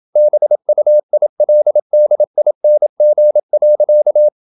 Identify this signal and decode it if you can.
Morse code, heard as a keyed tone, about 27 wpm, 600 Hz. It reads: BUILDING.
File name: Never 0027wpm